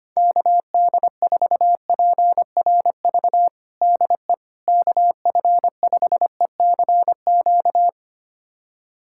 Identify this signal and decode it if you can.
Morse code, heard as a keyed tone, about 25 wpm, 700 Hz. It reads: KD4PRV DE KF5ECQ
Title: USA Callsign DE Callsign - 6 Characters - 25wpm